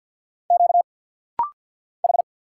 Morse code, heard as a keyed tone, about 50 wpm, 700 Hz. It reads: Q H
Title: Single Letters - Rapid-Fire 50wpm